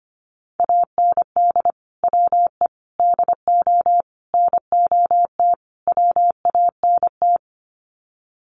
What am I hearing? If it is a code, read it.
Morse code, heard as a keyed tone, about 25 wpm, 700 Hz. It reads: AND WE DO NOT WANT
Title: Sets of 5 Words Encore2 25wpm